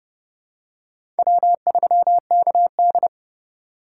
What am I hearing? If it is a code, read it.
Morse code, heard as a keyed tone, about 29 wpm, 700 Hz. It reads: W3KD
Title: USA Call Signs - 4 Characters - 3x Character Spacing - 30wpm